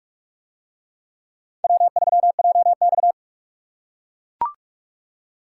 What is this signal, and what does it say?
Morse code, heard as a keyed tone, about 39 wpm, 700 Hz. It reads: W3JX